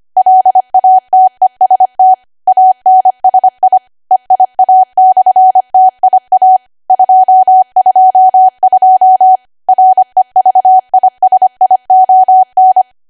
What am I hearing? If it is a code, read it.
Morse code, heard as a keyed tone, about 25 wpm, 750 Hz. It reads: LATEST ANSI EIA/TIA 222 REVISION
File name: Archive / 25 WPM / 25WPM